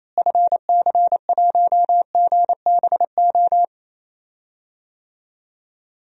Morse code, heard as a keyed tone, about 28 wpm, 700 Hz. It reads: FC1GBO
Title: DXCC-100 Call Signs - 6 Characters - Repeated 2x - 28wpm